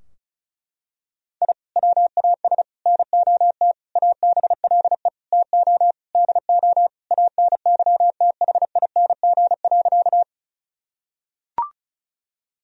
Morse code, heard as a keyed tone, about 36 wpm, 700 Hz. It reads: I WAS NOT ABLE TO DO ANYTHING.
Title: Sentences from Top 1000 Words - Review Part 2 - 35wpm